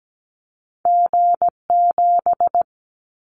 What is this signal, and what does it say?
Morse code, heard as a keyed tone, about 17 wpm, 700 Hz. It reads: G7
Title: Callsign Prefixes for Top 25 DXCC - Repeated 3x - 17wpm